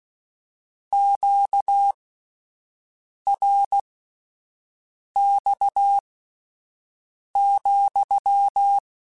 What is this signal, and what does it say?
Morse code, heard as a keyed tone, about 16 wpm, spaced as for 5 wpm, 800 Hz. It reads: QRX,